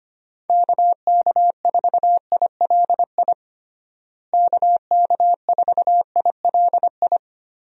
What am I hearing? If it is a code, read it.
Morse code, heard as a keyed tone, about 25 wpm, 700 Hz. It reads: KK4ILI KK4ILI
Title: USA Call Signs - 6 Characters - Repeated 3x - 25wpm